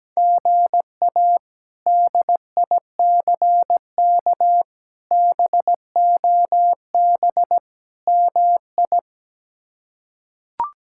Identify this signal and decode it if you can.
Morse code, heard as a keyed tone, about 17 wpm, 700 Hz. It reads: GA DICK BOB MI
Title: K1USN SST QSO Practice - 17wpm